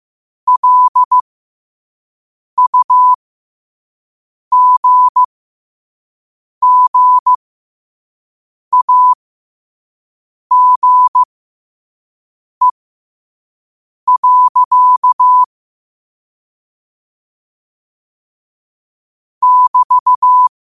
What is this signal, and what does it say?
Morse code, heard as a keyed tone, about 15 wpm, spaced as for 5 wpm, 1000 Hz. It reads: LUGGAGE. <BT>
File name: qotdfiles / 5wpm